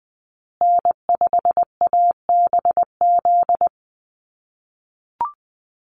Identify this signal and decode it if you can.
Morse code, heard as a keyed tone, about 20 wpm, 700 Hz. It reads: N5ABZ
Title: USA Call Signs Repeated 2x 20wpm